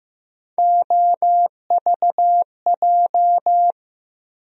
Morse code, heard as a keyed tone, about 15 wpm, 700 Hz. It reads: OVJ
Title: USA Call Sign Postfixes - 3 Letters 15wpm